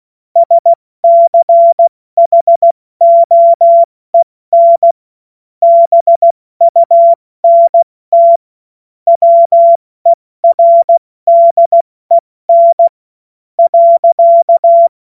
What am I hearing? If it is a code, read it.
Morse code, heard as a keyed tone, about 16 wpm, spaced as for 14 wpm, 680 Hz. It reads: SCHOEN BUNT WERDEN .
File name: Witz-14-Oktober-2025-70-F80